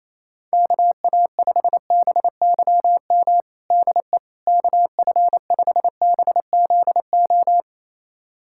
Morse code, heard as a keyed tone, about 27 wpm, 700 Hz. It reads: KA5BYM DE KF5BZO
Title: USA Callsign DE Callsign 28wpm